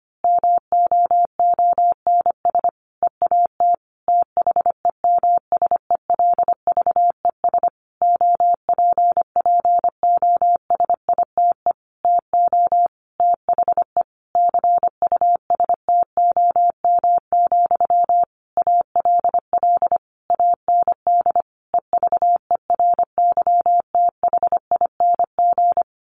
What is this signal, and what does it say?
Morse code, heard as a keyed tone, about 25 wpm, 700 Hz. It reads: MOONS EAT THEMSELVES OPPOSITE TO THE CUSTOM, ALL AND EVERYTHING
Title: A Sound Of Thunder 25wpm